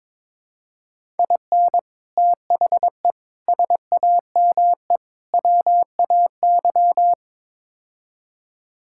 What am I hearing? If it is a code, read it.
Morse code, heard as a keyed tone, about 22 wpm, 700 Hz. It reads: IN THE SAME WAY
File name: practice / 22wpm / Build A Phrase - 5-Word Phrases - 22wpm